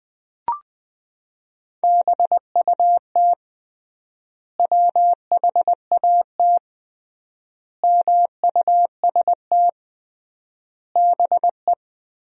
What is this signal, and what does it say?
Morse code, heard as a keyed tone, about 20 wpm, 700 Hz. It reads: BUT WHAT MUST BE
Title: 3x Word-Spacing - Sentences from Top 500 Words - 20wpm